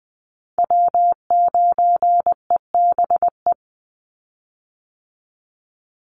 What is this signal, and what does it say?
Morse code, heard as a keyed tone, about 20 wpm, 700 Hz. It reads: W9EBE
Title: Call Signs Repeated 3x - 20wpm